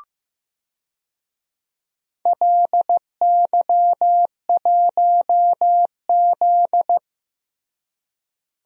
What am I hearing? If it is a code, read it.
Morse code, heard as a keyed tone, about 15 wpm, 700 Hz. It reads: LY1Z
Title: DXCC-100 Call Signs - 4 Characters - Repeated 2x - 15wpm